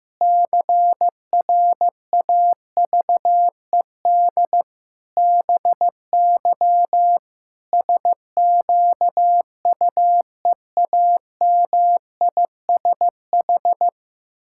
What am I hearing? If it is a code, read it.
Morse code, heard as a keyed tone, about 15 wpm, 700 Hz. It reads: CRAVED BY SQUEAMISH
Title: Pangram Challenge v2 - 15wpm